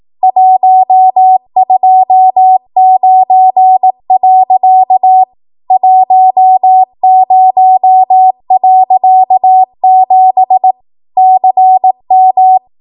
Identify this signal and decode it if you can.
Morse code, heard as a keyed tone, about 18 wpm, 750 Hz. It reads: 129. 10.7 CM
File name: Archive / 18 WPM / 18WPM